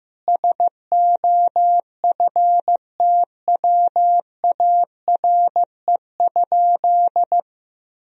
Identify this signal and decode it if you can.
Morse code, heard as a keyed tone, about 15 wpm, 700 Hz. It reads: SOFTWARE?
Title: Sentences from Top 1000 Words - Review Part 3 - 15wpm